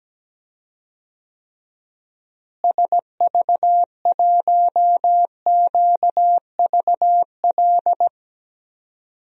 Morse code, heard as a keyed tone, about 17 wpm, 700 Hz. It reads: SV1QVL
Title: DXCC-100 Call Signs - 6 Characters - Repeated 2x - 17wpm